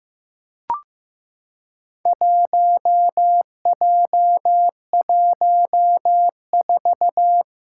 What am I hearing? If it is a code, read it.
Morse code, heard as a keyed tone, about 15 wpm, 700 Hz. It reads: 1J14
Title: Four Challenging Letter-Number Combinations 15wpm